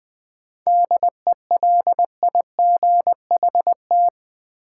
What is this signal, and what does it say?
Morse code, heard as a keyed tone, about 20 wpm, 700 Hz. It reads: DELIGHT
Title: 7 Letter Words - First 3 Letters Given - 20wpm